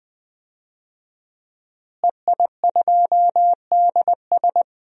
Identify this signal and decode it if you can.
Morse code, heard as a keyed tone, about 20 wpm, 700 Hz. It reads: EI2DS
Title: DXCC-100 Call Signs - 5 Characters - Repeated 3x - 20wpm